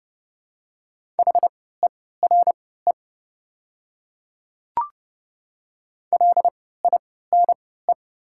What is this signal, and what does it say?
Morse code, heard as a keyed tone, about 30 wpm, spaced as for 17 wpm, 700 Hz. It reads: HERE LINE
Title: 3x Character Spacing - Top 300 Words - 30wpm